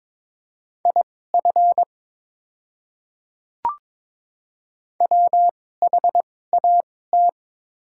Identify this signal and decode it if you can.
Morse code, heard as a keyed tone, about 22 wpm, spaced as for 16 wpm, 700 Hz. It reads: IF WHAT
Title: 2x Character Spacing - Top 100 Words - 22wpm